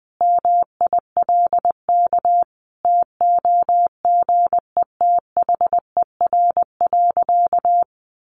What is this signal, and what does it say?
Morse code, heard as a keyed tone, about 20 wpm, 700 Hz. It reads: MILK TOGETHER.
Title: Sentences from Top 1500 Words 20wpm